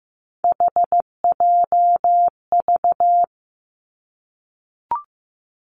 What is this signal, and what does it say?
Morse code, heard as a keyed tone, about 15 wpm, 700 Hz. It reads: HJV
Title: Three Challenging Letter-Number Combinations 15wpm